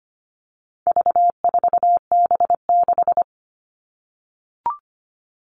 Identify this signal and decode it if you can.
Morse code, heard as a keyed tone, about 25 wpm, 700 Hz. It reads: V4B6
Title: All Cut Numbers - 4-Digit 25wpm